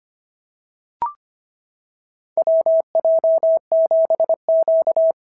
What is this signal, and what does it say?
Morse code, heard as a keyed tone, about 25 wpm, 650 Hz. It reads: WJ7Q
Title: SOTA QSO Practice 25wpm